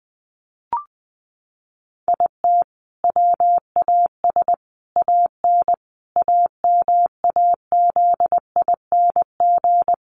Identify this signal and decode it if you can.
Morse code, heard as a keyed tone, about 20 wpm, 700 Hz. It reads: IT WAS AN AMAZING
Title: Sentences from Top 1500 Words 20wpm